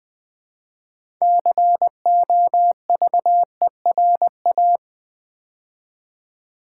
Morse code, heard as a keyed tone, about 20 wpm, 700 Hz. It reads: COVERA
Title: Build A Word - 3 Letters At A Time - 8-Letter Words - 20wpm